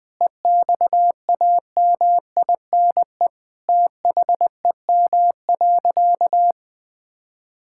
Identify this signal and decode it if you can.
Morse code, heard as a keyed tone, about 20 wpm, 700 Hz. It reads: EXAMINE THEM.
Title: Sentences from Top 1600 Words 20wpm